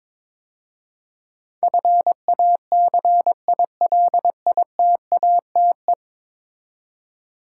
Morse code, heard as a keyed tone, about 23 wpm, 700 Hz. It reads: FACILITATE
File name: practice / 22wpm / 10 Letter Words - First 3 Letters Given - 22wpm